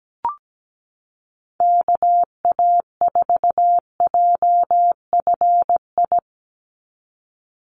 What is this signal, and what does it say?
Morse code, heard as a keyed tone, about 17 wpm, 700 Hz. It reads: KA4JFI
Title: USA Call Signs 17wpm